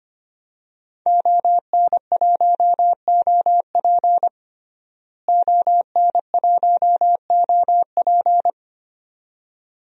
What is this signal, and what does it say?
Morse code, heard as a keyed tone, about 25 wpm, 700 Hz. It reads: ON1OP ON1OP
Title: DXCC-100 Call Signs - 5 Characters - Repeated 2x - 25wpm